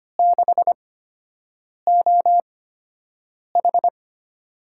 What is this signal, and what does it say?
Morse code, heard as a keyed tone, about 25 wpm, spaced as for 7 wpm, 700 Hz. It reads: 6OH
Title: USA Call Signs - 4 Characters - 8x Character Spacing - 25wpm